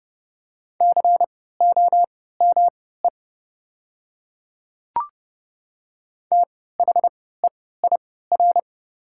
Morse code, heard as a keyed tone, about 30 wpm, spaced as for 17 wpm, 700 Hz. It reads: COME THEIR
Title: 3x Character Spacing - Top 100 Words - 30wpm